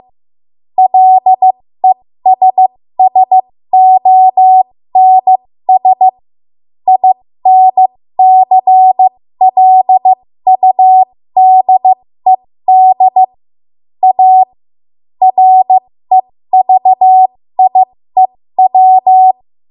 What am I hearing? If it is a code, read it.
Morse code, heard as a keyed tone, about 15 wpm, spaced as for 13 wpm, 750 Hz. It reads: LESSONS INCLUDED A REVIEW
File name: Morse / 13WPM